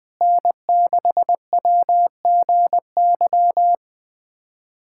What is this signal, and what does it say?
Morse code, heard as a keyed tone, about 21 wpm, 700 Hz. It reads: N6WGY
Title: USA Callsign DE Callsign 20wpm